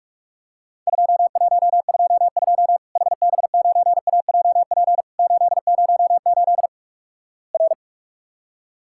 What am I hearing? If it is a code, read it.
Morse code, heard as a keyed tone, about 48 wpm, 700 Hz. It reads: 2122 HB9AJP 808 R
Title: Worked All Europe DX Contest - QTC Exchange - 45wpm